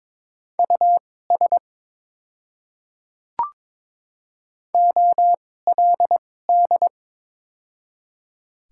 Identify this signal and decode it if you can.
Morse code, heard as a keyed tone, about 22 wpm, spaced as for 16 wpm, 700 Hz. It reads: US OLD
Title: 2x Character Spacing - Top 200 Words - 22wpm